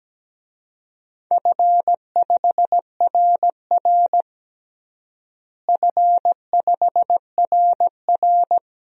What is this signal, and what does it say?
Morse code, heard as a keyed tone, about 17 wpm, 700 Hz. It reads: F5RR F5RR
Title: DXCC-100 Call Signs - 4 Characters - Repeated 2x - 17wpm